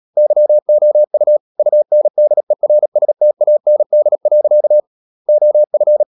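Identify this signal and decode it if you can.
Morse code, heard as a keyed tone, about 37 wpm, 600 Hz. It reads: YOU UNDERSTAND. OF
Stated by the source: Morse code practice at 35–39 wpm sending English text of Ken Follett's novel Never, from CW med Gnister.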